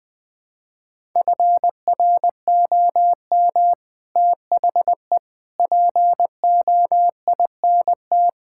Morse code, heard as a keyed tone, about 20 wpm, 700 Hz. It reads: FROM THE POINT
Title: Sentences with 5-Word Phrases - 20wpm